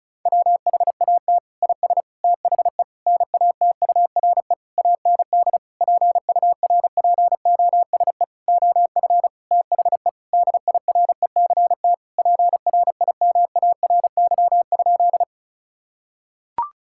Morse code, heard as a keyed tone, about 36 wpm, 700 Hz. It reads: WHAT IS THE NATURE AND PURPOSE OF THE DIRECT PRIMARY?
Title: Sentences from Top 1000 Words - Review Part 5 - 35wpm